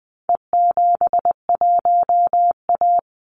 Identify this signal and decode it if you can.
Morse code, heard as a keyed tone, about 20 wpm, 700 Hz. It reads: E71A
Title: Call Signs Repeated 3x - 20wpm